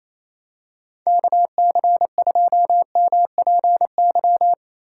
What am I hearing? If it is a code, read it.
Morse code, heard as a keyed tone, about 27 wpm, 700 Hz. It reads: KC2MPY
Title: USA Call Signs - 2x Character Spacing - 28wpm